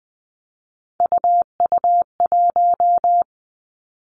Morse code, heard as a keyed tone, about 20 wpm, 700 Hz. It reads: UU1
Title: Three Challenging Letter-Number Combinations 20wpm